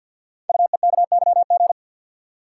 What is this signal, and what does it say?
Morse code, heard as a keyed tone, about 50 wpm, 700 Hz. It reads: UEKYG
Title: Code Groups - 5 Letters - 50wpm